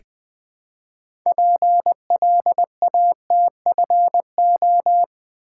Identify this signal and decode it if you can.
Morse code, heard as a keyed tone, about 20 wpm, 700 Hz. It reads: PLATFO
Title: Build A Word - 2 Letters At A Time - 8-Letter Words - 20wpm